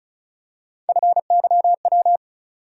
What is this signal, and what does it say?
Morse code, heard as a keyed tone, about 36 wpm, 700 Hz. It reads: FYW
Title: USA Call Sign Postfixes - 3 Letters 35wpm